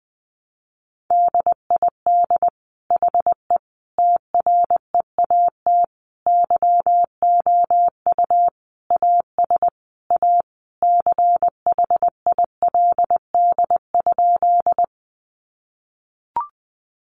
Morse code, heard as a keyed tone, about 19 wpm, 700 Hz. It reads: DID HE TREAT YOU AS A CHILD?
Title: Sentences from Top 1300 Words 20wpm